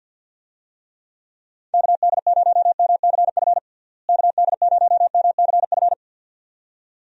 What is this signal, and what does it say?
Morse code, heard as a keyed tone, about 50 wpm, 700 Hz. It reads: KD0MCF KD0MCF
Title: USA Call Signs - 6 Characters - Repeated 2x - 50wpm